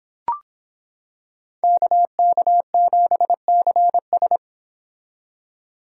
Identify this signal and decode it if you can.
Morse code, heard as a keyed tone, about 25 wpm, 700 Hz. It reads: KK7CS